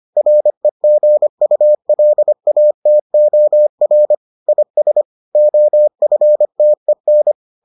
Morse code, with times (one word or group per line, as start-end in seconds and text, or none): 0.2-4.2 REGULATOR
4.5-5.0 IS
5.4-7.3 OFTEN